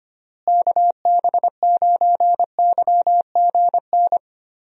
Morse code, heard as a keyed tone, about 25 wpm, 700 Hz. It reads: KB9YGN
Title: USA Call Signs - 6 Characters - Repeated 3x - 25wpm